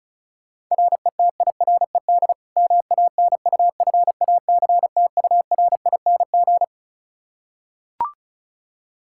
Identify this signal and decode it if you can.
Morse code, heard as a keyed tone, about 36 wpm, 700 Hz. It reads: RETIRED MANUFACTURING
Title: K1USN SST QSO Practice 35wpm